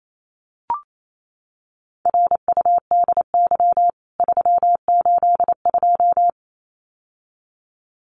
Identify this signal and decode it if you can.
Morse code, heard as a keyed tone, about 28 wpm, 700 Hz. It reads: RUDY 382
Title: CWT Exchange 28wpm